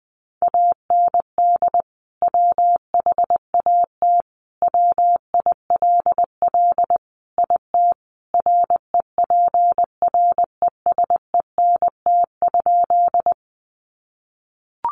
Morse code, heard as a keyed tone, about 20 wpm, 700 Hz. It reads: AND WHAT WILL IT REPRESENT?
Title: Sentences from Top 1200 Words 20wpm